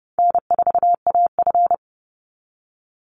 Morse code, heard as a keyed tone, about 30 wpm, 700 Hz. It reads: N4AF
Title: Easy Call Signs - 30wpm